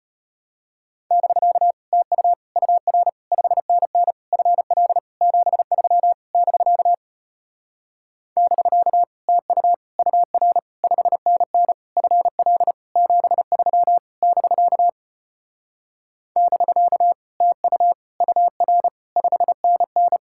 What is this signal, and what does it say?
Morse code, heard as a keyed tone, about 33 wpm, 700 Hz. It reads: <BK> TU UR 5NN FL 73 <BK> <BK> TU UR 5NN FL 73 <BK> <BK> TU UR 5NN